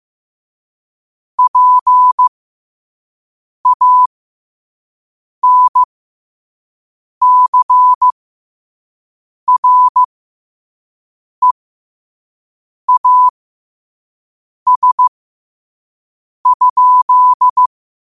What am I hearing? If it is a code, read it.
Morse code, heard as a keyed tone, about 15 wpm, spaced as for 5 wpm, 1000 Hz. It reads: PANCREAS?
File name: qotdfiles / 5wpm